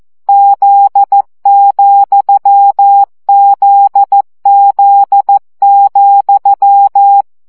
Morse code, heard as a keyed tone, about 14 wpm, 800 Hz. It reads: Z,ZZ,
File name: CW gyakorlas z 7 vesszo 12 baud